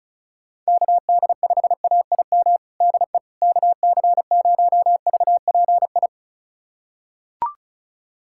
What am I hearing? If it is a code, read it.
Morse code, heard as a keyed tone, about 34 wpm, 700 Hz. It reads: KD5AIM DE KC0VPI
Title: USA Callsign DE Callsign 35wpm